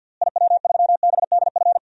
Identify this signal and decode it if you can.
Morse code, heard as a keyed tone, about 51 wpm, 700 Hz. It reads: IW3BDF
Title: DXCC-100 Call Signs - 6 Characters - Repeated 3x - 50wpm